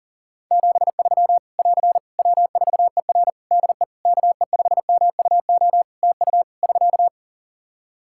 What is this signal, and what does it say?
Morse code, heard as a keyed tone, about 37 wpm, 700 Hz. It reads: 73 <AR> W4ER DE KE5MUO TU <SK>
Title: K1USN SST QSO Practice 40wpm